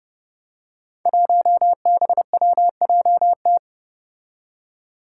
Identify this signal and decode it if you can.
Morse code, heard as a keyed tone, about 30 wpm, 700 Hz. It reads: 1BWJT
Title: Five Challenging Letter-Number Combinations 30wpm